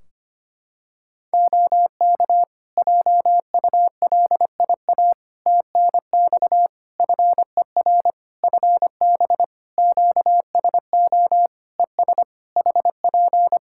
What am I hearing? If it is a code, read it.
Morse code, heard as a keyed tone, about 25 wpm, 700 Hz. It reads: OK JULIA TNX FER FB QSO ES HP
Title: Practice QSO - Exchange 4 - Ending - 25wpm